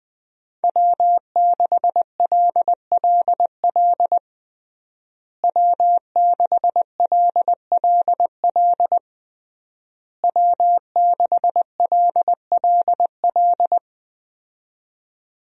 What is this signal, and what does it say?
Morse code, heard as a keyed tone, about 21 wpm, 700 Hz. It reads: W6LLL W6LLL W6LLL
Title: USA Call Signs - 5 Characters - Repeated 3x - 20wpm